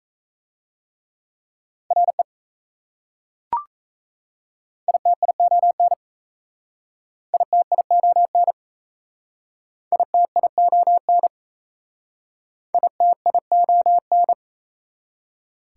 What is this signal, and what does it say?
Morse code, heard as a keyed tone, about 36 wpm, 700 Hz. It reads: RE ITION ITION ITION ITION